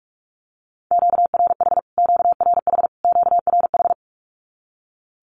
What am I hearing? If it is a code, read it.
Morse code, heard as a keyed tone, about 46 wpm, 700 Hz. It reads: QRH QRH QRH
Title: Q-Codes Repeated 3x 45wpm